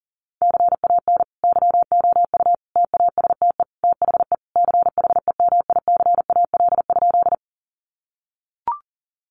Morse code, heard as a keyed tone, about 40 wpm, 700 Hz. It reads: CAN YOU TASTE THE CHEMICAL?
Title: Sentences from Top 1200 Words 40wpm